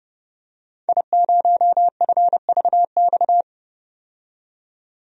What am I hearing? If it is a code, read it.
Morse code, heard as a keyed tone, about 30 wpm, 700 Hz. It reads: I0FVX
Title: DXCC-100 Call Signs - 5 Characters - 30wpm